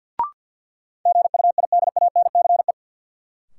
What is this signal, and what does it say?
Morse code, heard as a keyed tone, about 52 wpm, 700 Hz. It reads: GUIDANCE
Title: Sentences from Top 1600 Words - New Word Drill - 50wpm